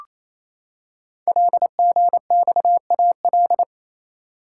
Morse code, heard as a keyed tone, about 27 wpm, 700 Hz. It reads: LGXAL
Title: Code Groups - 5 Letters - 28wpm